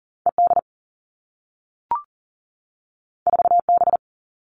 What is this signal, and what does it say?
Morse code, heard as a keyed tone, about 42 wpm, 700 Hz. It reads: ED 4B
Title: All Cut Numbers - 2-Digit 40wpm